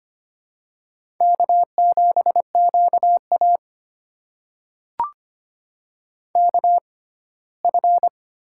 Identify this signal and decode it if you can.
Morse code, heard as a keyed tone, about 25 wpm, spaced as for 17 wpm, 700 Hz. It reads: K7QA KF
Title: USA Call Signs - 6x Character Spacing - 25wpm